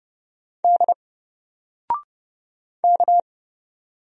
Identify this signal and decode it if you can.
Morse code, heard as a keyed tone, about 30 wpm, 700 Hz. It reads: D K
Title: Single Letter-Number Alt 30wpm